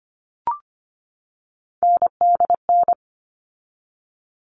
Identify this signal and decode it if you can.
Morse code, heard as a keyed tone, about 25 wpm, 700 Hz. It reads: NDN
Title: All Cut Numbers - 3-Digit 25wpm